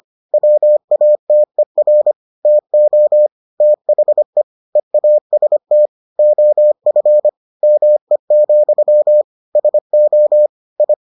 Morse code, heard as a keyed tone, about 25 wpm, 600 Hz. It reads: WATER TO THE EAST OF ME, SO I